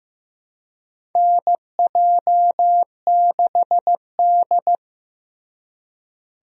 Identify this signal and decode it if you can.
Morse code, heard as a keyed tone, about 15 wpm, 700 Hz. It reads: NJ6D
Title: USA Call Signs - 4 Characters - Repeated 2x - 15wpm